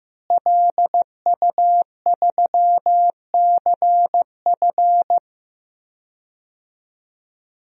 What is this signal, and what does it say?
Morse code, heard as a keyed tone, about 15 wpm, 700 Hz. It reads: LU3CF
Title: DXCC-100 Call Signs - 5 Characters - Repeated 3x - 15wpm